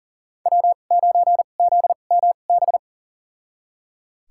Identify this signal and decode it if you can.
Morse code, heard as a keyed tone, about 37 wpm, spaced as for 28 wpm, 700 Hz. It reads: W9ZMB
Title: USA Call Signs - 5 Characters - 2x Character Spacing - 40wpm